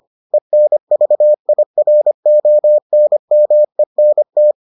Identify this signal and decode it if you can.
Morse code, heard as a keyed tone, about 25 wpm, 600 Hz. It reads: ENVIRONMENT